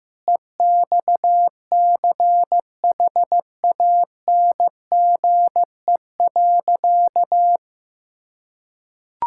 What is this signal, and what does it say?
Morse code, heard as a keyed tone, about 15 wpm, 700 Hz. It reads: EXCHANGE.
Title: Sentences from Top 1000 Words - Review Part 6 - 15wpm